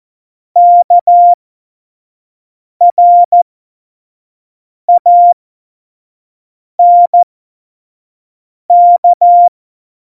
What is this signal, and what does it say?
Morse code, heard as a keyed tone, about 14 wpm, spaced as for 5 wpm, 700 Hz. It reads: KRANK
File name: Morsekurs / Witze / Witz 09. Januar 2018, 25 F70